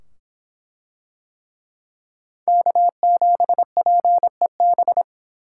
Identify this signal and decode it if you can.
Morse code, heard as a keyed tone, about 24 wpm, 700 Hz. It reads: K7PEB